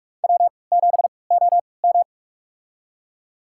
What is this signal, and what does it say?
Morse code, heard as a keyed tone, about 47 wpm, spaced as for 26 wpm, 700 Hz. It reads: W7OM
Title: USA Call Signs - 4 Characters - 3x Character Spacing - 45wpm